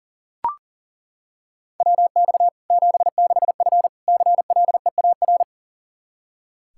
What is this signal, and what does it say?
Morse code, heard as a keyed tone, about 37 wpm, 700 Hz. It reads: WX 76F CLEAR
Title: K1USN SST QSO Practice 40wpm